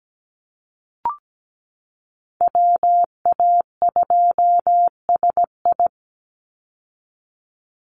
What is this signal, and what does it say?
Morse code, heard as a keyed tone, about 17 wpm, 700 Hz. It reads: WA2SI
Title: Call Signs Repeated 3x - 17wpm